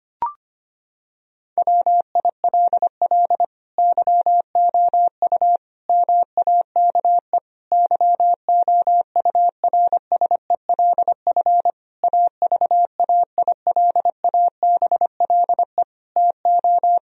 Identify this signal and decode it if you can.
Morse code, heard as a keyed tone, about 25 wpm, 700 Hz. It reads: WILL YOU MAKE YOURSELF AVAILABLE TO
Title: Sentences from Top 1000 Words - Review Part 3 - 25wpm